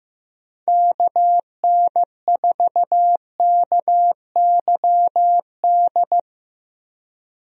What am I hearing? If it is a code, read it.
Morse code, heard as a keyed tone, about 15 wpm, 700 Hz. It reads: KN4KYD
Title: USA Call Signs - 6 Characters - Repeated 2x - 15wpm